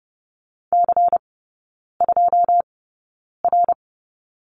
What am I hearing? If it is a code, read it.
Morse code, heard as a keyed tone, about 30 wpm, spaced as for 9 wpm, 700 Hz. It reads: C2R
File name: Lesson 16e - Letter c - Three Characters 30wpm